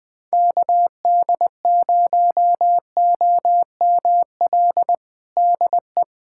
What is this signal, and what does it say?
Morse code, heard as a keyed tone, about 20 wpm, 700 Hz. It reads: KD0OML DE
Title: USA Callsign DE Callsign 20wpm